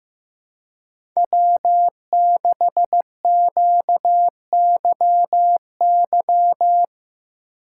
Morse code, heard as a keyed tone, about 15 wpm, 700 Hz. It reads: W6QYY
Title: USA Call Signs - 5 Characters - 2x Character Spacing - 15wpm